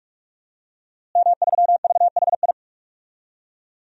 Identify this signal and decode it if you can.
Morse code, heard as a keyed tone, about 46 wpm, 700 Hz. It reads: M3VHI
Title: DXCC-100 Call Signs - 5 Characters - Repeated 2x - 45wpm